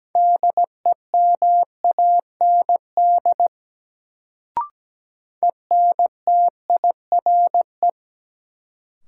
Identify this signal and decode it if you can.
Morse code, heard as a keyed tone, about 17 wpm, 700 Hz. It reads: DEMAND ENTIRE
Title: Sentences from Top 800 Words - New Word Drill - 17wpm